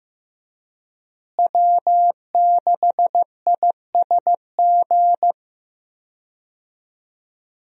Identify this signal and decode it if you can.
Morse code, heard as a keyed tone, about 15 wpm, 700 Hz. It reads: W6ISG
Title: USA Call Signs - 5 Characters - Repeated 2x - 15wpm